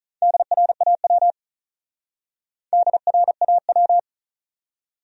Morse code, heard as a keyed tone, about 44 wpm, 700 Hz. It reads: DRAW DRAW
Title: Speed-Racing Top 500 Words 35wpm